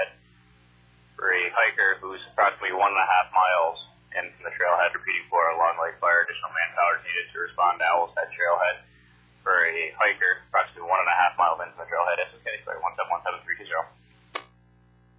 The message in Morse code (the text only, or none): none